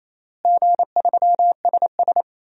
Morse code, heard as a keyed tone, about 28 wpm, 700 Hz. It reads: G3SS
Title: DXCC-100 Call Signs - 4 Characters - Repeated 3x - 28wpm